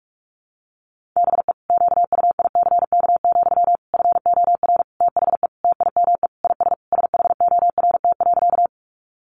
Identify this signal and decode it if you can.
Morse code, heard as a keyed tone, about 44 wpm, 700 Hz. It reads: BE QUICK, FOR THE TIME IS SHORT.
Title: Sentences from Top 500 Words 45wpm